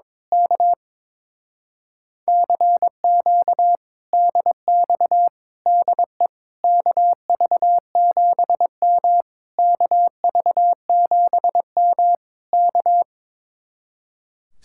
Morse code, heard as a keyed tone, about 22 wpm, 700 Hz. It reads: K CQ DX DE KV7M KV7M K
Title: Speed-Racing Common QSO Phrases - 22wpm